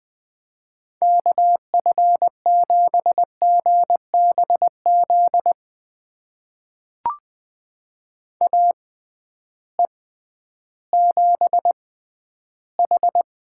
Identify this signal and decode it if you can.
Morse code, heard as a keyed tone, about 21 wpm, spaced as for 12 wpm, 700 Hz. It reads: KF7GBZ AE7H
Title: USA Call Signs - 6x Character Spacing - 20wpm